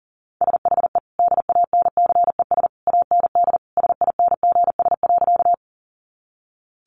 Morse code, heard as a keyed tone, about 40 wpm, 700 Hz. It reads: SHE DANCES AND SINGS.
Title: Easy Phrases 40wpm